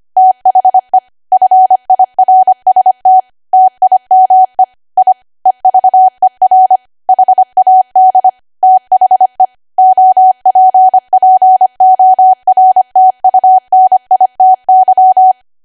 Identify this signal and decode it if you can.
Morse code, heard as a keyed tone, about 25 wpm, 750 Hz. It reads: THE FIRST TIME I EVER HAD THE OPPORTUNITY
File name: Archive / 25 WPM / 25WPM